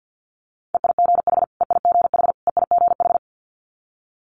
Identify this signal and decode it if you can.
Morse code, heard as a keyed tone, about 50 wpm, 700 Hz. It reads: EIGH EIGH EIGH
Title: Multi-letter Phonemes Repeated 3x 50wpm